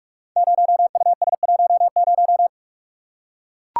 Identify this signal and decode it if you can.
Morse code, heard as a keyed tone, about 44 wpm, 700 Hz. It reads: 0US10
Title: Code Groups - 5 Letter:Numbers - 45wpm